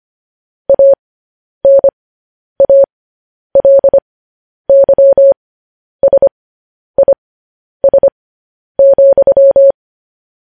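Morse code, heard as a keyed tone, about 25 wpm, spaced as for 10 wpm, 550 Hz. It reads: ANALYSIS,
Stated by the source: Morse code news headlines, likely sent at 10 WPM.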